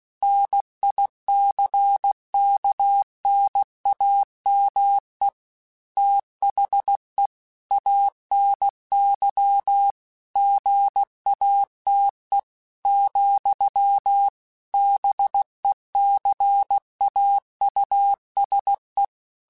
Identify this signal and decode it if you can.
Morse code, heard as a keyed tone, about 16 wpm, 800 Hz. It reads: NICKNAME THE ANY GATE , BECAUSE
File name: Technical Subjects / Technical Subjects-mp3s / OR Gate @15